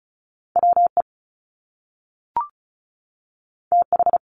Morse code, heard as a keyed tone, about 36 wpm, 700 Hz. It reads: WE TH
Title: Top Two Letter Combinations in Words 35wpm